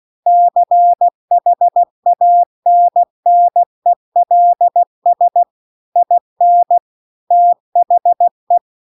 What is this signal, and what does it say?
Morse code, heard as a keyed tone, about 16 wpm, 700 Hz. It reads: CHANNELS IN THE
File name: CW-The-Big-Twoheaded-River-2-16WPM